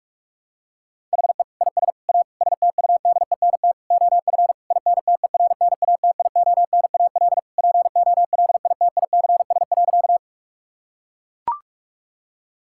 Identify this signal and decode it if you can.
Morse code, heard as a keyed tone, about 41 wpm, 700 Hz. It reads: HE IS A STUDENT OF INTERNATIONAL POLITICS.
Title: Sentences from Top 1000 Words - Review Part 6 - 45wpm